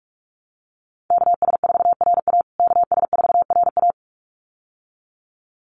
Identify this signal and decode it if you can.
Morse code, heard as a keyed tone, about 44 wpm, 700 Hz. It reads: KS4RA KS4RA
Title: USA Call Signs Repeated 2x 45wpm